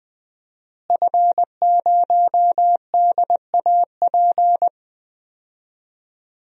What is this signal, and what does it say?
Morse code, heard as a keyed tone, about 20 wpm, 700 Hz. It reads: F0DAP
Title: DXCC-100 Call Signs - 5 Characters - Repeated 2x - 20wpm